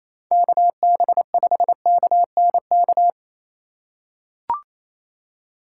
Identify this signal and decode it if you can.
Morse code, heard as a keyed tone, about 28 wpm, 700 Hz. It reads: KB5KNK
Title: USA Call Signs - 2x Character Spacing - 28wpm